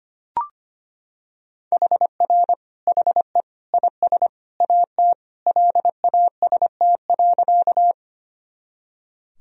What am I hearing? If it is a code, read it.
Morse code, heard as a keyed tone, about 25 wpm, 700 Hz. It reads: HR HE IS AT LAST.
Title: Sentences from Top 300 Words and Abbreviations 25wpm